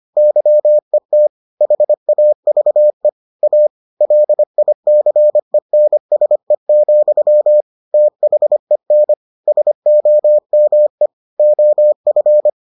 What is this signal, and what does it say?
Morse code, heard as a keyed tone, about 25 wpm, 600 Hz. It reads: YET HAVE A LICENSE, THEN SOME OF